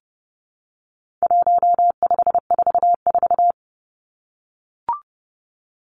Morse code, heard as a keyed tone, about 30 wpm, 700 Hz. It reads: 1544
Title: Numbers - 4-Digit 30wpm